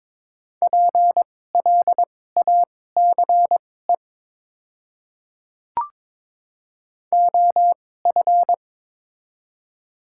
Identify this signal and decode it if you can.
Morse code, heard as a keyed tone, about 22 wpm, spaced as for 16 wpm, 700 Hz. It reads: PLACE OF
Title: 2x Character Spacing - Top 500 Words - 22wpm